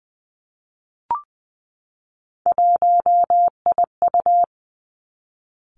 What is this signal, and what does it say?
Morse code, heard as a keyed tone, about 20 wpm, 700 Hz. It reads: 1IU
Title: Three Challenging Letter-Number Combinations 20wpm